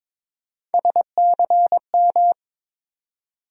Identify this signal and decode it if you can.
Morse code, heard as a keyed tone, about 22 wpm, 700 Hz. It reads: SCM
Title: USA Call Sign Postfixes - 3 Letters 22wpm